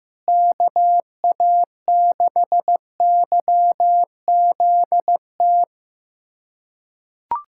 Morse code, heard as a keyed tone, about 15 wpm, 700 Hz. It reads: KA6YZT
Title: USA Callsign DE Callsign - 6 Characters - 15wpm